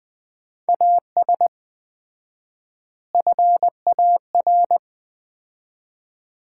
Morse code, heard as a keyed tone, about 19 wpm, 700 Hz. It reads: AS FAR
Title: 4x Word-Spacing - Sets of 4 Words - 20wpm